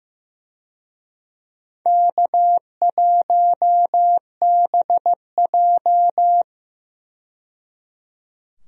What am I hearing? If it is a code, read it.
Morse code, heard as a keyed tone, about 15 wpm, 700 Hz. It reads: K1BJ
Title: USA Call Signs - 4 Characters - 15wpm